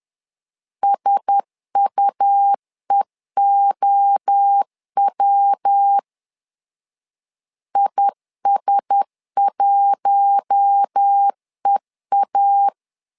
Morse code, heard as a keyed tone, about 10 wpm, 800 Hz. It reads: SUEOW IS1EA
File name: LEZIONI / 4 Q1S 10wpm 2133